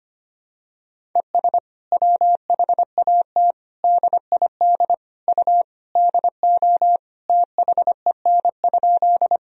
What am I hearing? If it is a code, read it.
Morse code, heard as a keyed tone, about 25 wpm, 700 Hz. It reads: ES WHAT DID U DO THEN?
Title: Sentences from Top 100 Words and Abbreviations 25wpm